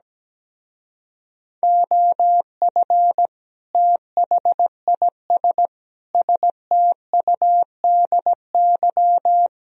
Morse code, heard as a keyed tone, about 17 wpm, 700 Hz. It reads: OF THIS STUDY
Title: Speed-Racing - Sets of 5 Words 17wpm